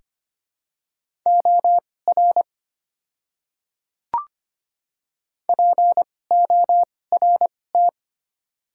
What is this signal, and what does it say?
Morse code, heard as a keyed tone, about 25 wpm, spaced as for 18 wpm, 690 Hz. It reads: OR PORT
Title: 2x Character Spacing - Top 300 Words - 25wpm